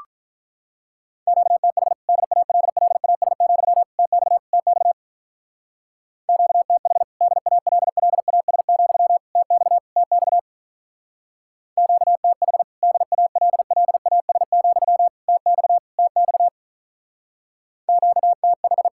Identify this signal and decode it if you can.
Morse code, heard as a keyed tone, about 45 wpm, 700 Hz. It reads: QTH DALLAS, TX TX QTH DALLAS, TX TX QTH DALLAS, TX TX QTH